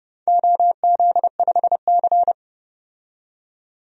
Morse code, heard as a keyed tone, about 32 wpm, 700 Hz. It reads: OZ5C
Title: DXCC-100 Call Signs - 4 Characters - 30wpm